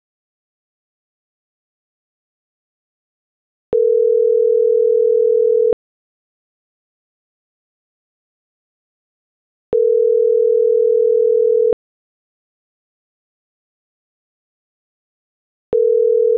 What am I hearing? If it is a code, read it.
Ringback tone (North America), 440 + 480 Hz.